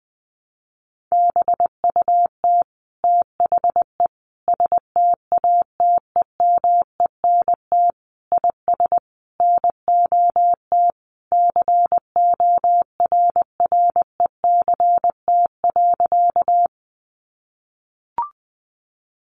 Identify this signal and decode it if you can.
Morse code, heard as a keyed tone, about 20 wpm, 700 Hz. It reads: BUT THE STATEMENT IS NOT CORRECT.
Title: Sentences from Top 1200 Words 20wpm